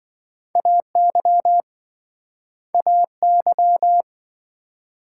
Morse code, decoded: AY AY

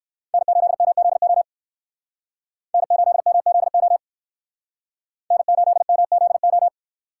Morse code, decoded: N8MZO N8MZO N8MZO

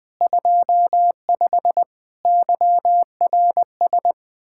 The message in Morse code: 25 YRS

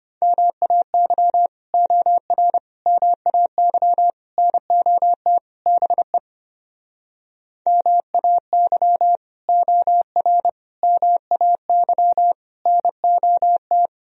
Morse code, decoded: MAY OR MAY NOT BE MAY OR MAY NOT

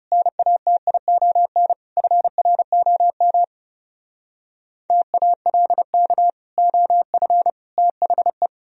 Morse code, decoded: NATION FROM TALK OF THE

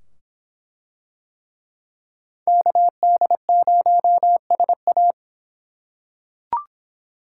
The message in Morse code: KD0SA

